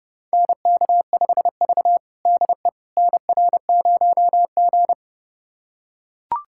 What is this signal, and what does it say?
Morse code, heard as a keyed tone, about 32 wpm, 700 Hz. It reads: NK5V DE NR0G
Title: USA Callsign DE Callsign - 4 Characters - 30wpm